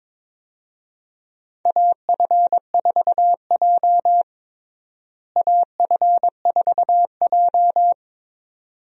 Morse code, heard as a keyed tone, about 22 wpm, 700 Hz. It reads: AF4J AF4J
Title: USA Call Signs - 4 Characters - Repeated 2x - 22wpm